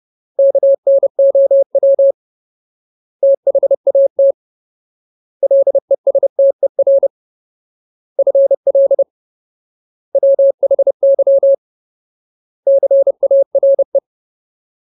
Morse code, heard as a keyed tone, about 30 wpm, 550 Hz. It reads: KNOW THAT LESTER FL WHY CARE
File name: Ultimate Word List / 30WPM Ultimate